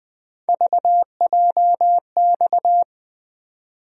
Morse code, heard as a keyed tone, about 20 wpm, 700 Hz. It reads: VJX
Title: USA Call Sign Postfixes - 3 Letters 20wpm